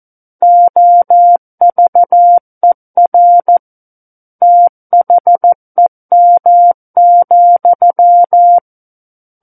Morse code, heard as a keyed tone, about 14 wpm, 700 Hz. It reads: OVER THEM,
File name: Book One / 05-Chapter 5 - 14 WPM